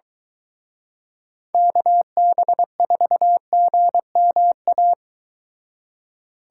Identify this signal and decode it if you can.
Morse code, heard as a keyed tone, about 22 wpm, 700 Hz. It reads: KB4GMA